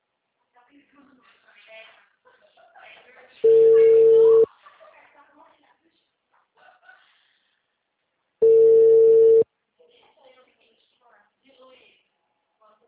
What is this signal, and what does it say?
Ringback tone (China, 450 Hz), with speech.